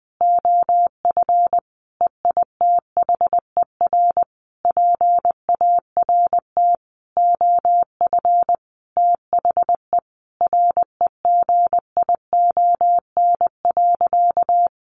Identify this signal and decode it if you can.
Morse code, heard as a keyed tone, about 20 wpm, 700 Hz. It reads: OF EITHER PART OF THE REGION.
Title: Sentences from Top 1500 Words 20wpm